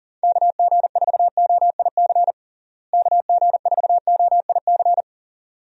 Morse code, heard as a keyed tone, about 40 wpm, 700 Hz. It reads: KG4OIC KG4OIC
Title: USA Call Signs - 6 Characters - Repeated 2x - 40wpm